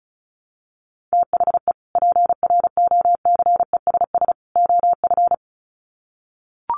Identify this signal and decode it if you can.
Morse code, heard as a keyed tone, about 34 wpm, 700 Hz. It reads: THE PROCESS OF
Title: Sets of 3 Words 35wpm